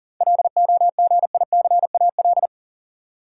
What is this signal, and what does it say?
Morse code, heard as a keyed tone, about 40 wpm, 700 Hz. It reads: LOGICAL
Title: Seven Letter Words - 40wpm